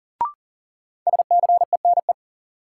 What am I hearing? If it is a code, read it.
Morse code, heard as a keyed tone, about 40 wpm, 700 Hz. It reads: SCENE
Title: Sentences from Top 1100 Words - New Word Drill Alt - 40wpm